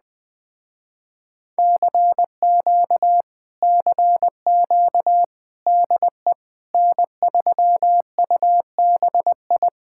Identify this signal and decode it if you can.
Morse code, heard as a keyed tone, about 20 wpm, 700 Hz. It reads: CQ CQ DE N3UBI